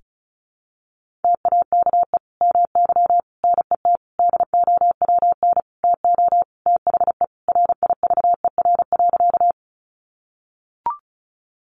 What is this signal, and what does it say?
Morse code, heard as a keyed tone, about 34 wpm, 700 Hz. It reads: TAKE MY NET DOWN TO THE RIVER.
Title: Sentences from Top 1400 Words 35wpm